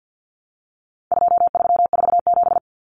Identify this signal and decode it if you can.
Morse code, heard as a keyed tone, about 48 wpm, 700 Hz. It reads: 2347
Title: Numbers - 4-Digit 50wpm